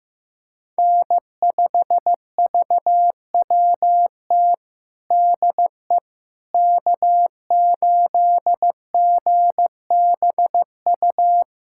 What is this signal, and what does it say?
Morse code, heard as a keyed tone, about 16 wpm, 700 Hz. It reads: N5VWT DE K8GBU
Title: USA Callsign DE Callsign - 5 Characters - 15wpm